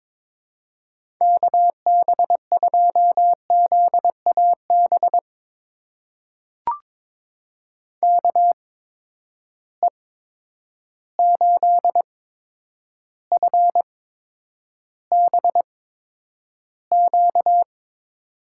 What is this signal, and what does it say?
Morse code, heard as a keyed tone, about 22 wpm, spaced as for 9 wpm, 700 Hz. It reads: KB2ZAB KE8FBQ
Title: USA Call Signs - 6 Characters - 8x Character Spacing - 22wpm